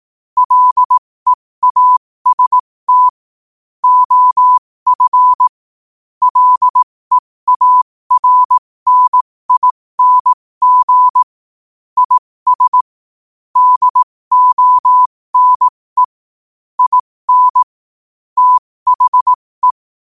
Morse code, heard as a keyed tone, about 18 wpm, spaced as for 15 wpm, 1000 Hz. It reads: LEAST OF LEARNING IS DONE IN THE
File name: qotdfiles / 15wpm